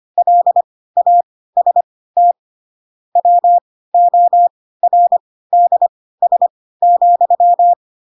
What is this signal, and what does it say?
Morse code, heard as a keyed tone, about 25 wpm, spaced as for 16 wpm, 700 Hz. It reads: LAST WORDS,
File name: Pinocchio / Pinocchio ch4 10 25 ik1hge